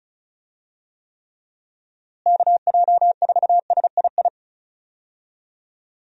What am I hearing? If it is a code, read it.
Morse code, heard as a keyed tone, about 36 wpm, 700 Hz. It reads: KJ4SII
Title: USA Call Signs - 6 Characters - Repeated 2x - 35wpm